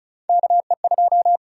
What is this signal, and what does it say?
Morse code, heard as a keyed tone, about 34 wpm, 700 Hz. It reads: KE2
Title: USA Call Sign Prefixes - Repeated 2x - 35wpm